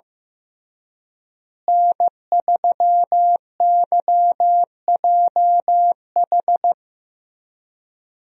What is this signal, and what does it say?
Morse code, heard as a keyed tone, about 16 wpm, 700 Hz. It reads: N3YJH
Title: USA Call Signs - 5 Characters - Repeated 3x - 15wpm